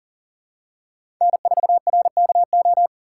Morse code, heard as a keyed tone, about 40 wpm, 700 Hz. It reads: N4RKO
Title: USA Call Signs - 5 Characters - 4x Character Spacing - 40wpm